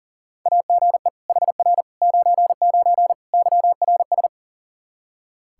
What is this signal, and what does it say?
Morse code, heard as a keyed tone, about 40 wpm, 700 Hz. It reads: AGE HR 99 YRS
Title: Practice QSO - Exchange 3 - More Info - 40wpm